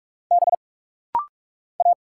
Morse code, heard as a keyed tone, about 46 wpm, 700 Hz. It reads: B A
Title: Single Letter-Number Alt - Rapid-Fire 45wpm